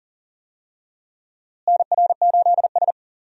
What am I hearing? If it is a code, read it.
Morse code, heard as a keyed tone, about 40 wpm, 700 Hz. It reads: NR8S
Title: USA Call Signs - 4 Characters - Repeated 2x - 40wpm